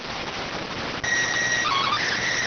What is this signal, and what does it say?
SSTV (slow-scan television), PD 290.